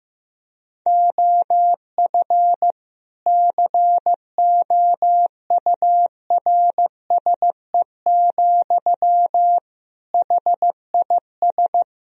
Morse code, heard as a keyed tone, about 15 wpm, 700 Hz. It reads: OF COURSE, HIS
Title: Sentences from Top 1000 Words - Review Part 3 - 15wpm